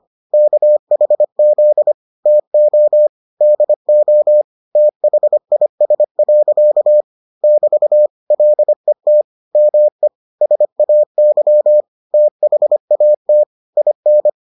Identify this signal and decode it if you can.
Morse code, heard as a keyed tone, about 25 wpm, 600 Hz. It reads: KHZ TO DO THIS. <BT> LET ME SAY THAT IN